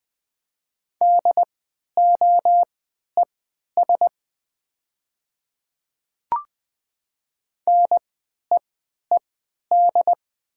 Morse code, heard as a keyed tone, about 20 wpm, spaced as for 11 wpm, 700 Hz. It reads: DOES NEED
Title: 3x Character Spacing - Top 300 Words - 20wpm